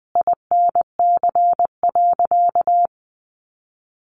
Morse code, heard as a keyed tone, about 20 wpm, 700 Hz. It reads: INC.
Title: A Sound Of Thunder 20wpm